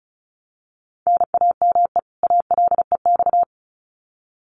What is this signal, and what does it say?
Morse code, heard as a keyed tone, about 34 wpm, 700 Hz. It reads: NAME ALEX
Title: Common QSO Phrases 35wpm